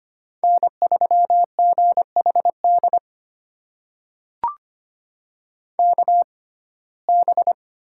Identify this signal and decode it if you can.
Morse code, heard as a keyed tone, about 25 wpm, spaced as for 18 wpm, 700 Hz. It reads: N3GHD KB
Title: USA Call Signs - 6x Character Spacing - 25wpm